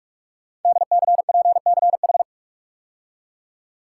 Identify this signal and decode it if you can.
Morse code, heard as a keyed tone, about 44 wpm, 700 Hz. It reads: DCPCH
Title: Code Groups - 5 Letters - 45wpm